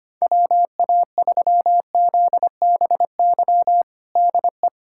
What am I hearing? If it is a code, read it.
Morse code, heard as a keyed tone, about 25 wpm, 700 Hz. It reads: WA3ZBY DE
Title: USA Callsign DE Callsign 25wpm